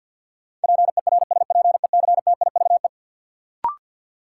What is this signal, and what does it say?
Morse code, heard as a keyed tone, about 52 wpm, 700 Hz. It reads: PERSPECTIVE
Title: Sentences from Top 1200 Words - New Word Drill - 50wpm